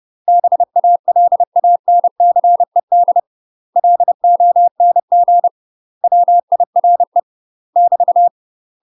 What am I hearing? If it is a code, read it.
Morse code, heard as a keyed tone, about 30 wpm, 700 Hz. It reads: BALANCED LONG WIRE <BT>
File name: UK MASTER QSOs TRAINING/30WPM